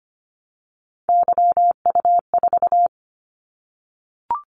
Morse code, heard as a keyed tone, about 25 wpm, 700 Hz. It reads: YU4
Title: Callsign Prefixes for Top 25 DXCC - 25wpm